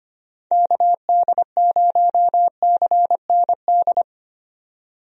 Morse code, heard as a keyed tone, about 25 wpm, 700 Hz. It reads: KD0CND
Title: USA Call Signs - 6 Characters - 25wpm